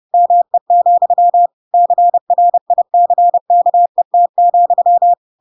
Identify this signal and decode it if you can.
Morse code, heard as a keyed tone, about 30 wpm, 700 Hz. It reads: ME, CRICKET,
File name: Pinocchio / Pinocchio ch4 30 30 ik1hge